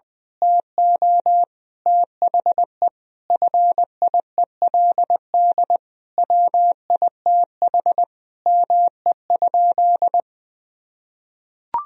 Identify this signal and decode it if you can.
Morse code, heard as a keyed tone, about 21 wpm, 700 Hz. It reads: TO THE FIELD WITH ME?
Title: Sentences from Top 1000 Words - Review Part 2 - 20wpm